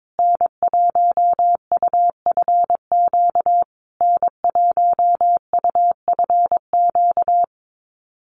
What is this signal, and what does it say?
Morse code, heard as a keyed tone, about 22 wpm, 700 Hz. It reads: N1UFQ N1UFQ
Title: USA Call Signs Repeated 3x 22wpm